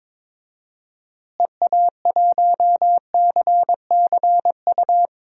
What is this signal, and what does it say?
Morse code, heard as a keyed tone, about 22 wpm, 700 Hz. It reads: EA1CCU
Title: DXCC-100 Call Signs - 6 Characters - Repeated 3x - 22wpm